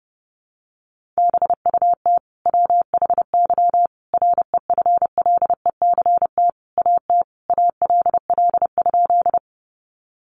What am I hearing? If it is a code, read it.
Morse code, heard as a keyed tone, about 30 wpm, 700 Hz. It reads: BUT WHY REFLECT AT ALL?
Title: Sentences from Top 1500 Words 30wpm